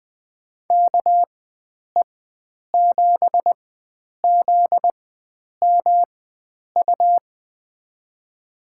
Morse code, heard as a keyed tone, about 20 wpm, spaced as for 9 wpm, 700 Hz. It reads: KE7ZMU
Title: USA Call Signs - 6 Characters - 4x Character Spacing - 20wpm